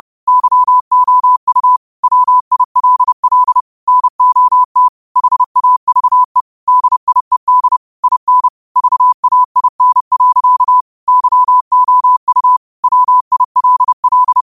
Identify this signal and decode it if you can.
Morse code, heard as a keyed tone, about 30 wpm, 1000 Hz. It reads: YOU WILL NOT HAVE DIED IN VAIN. YOU WILL
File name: qotdfiles / 30wpm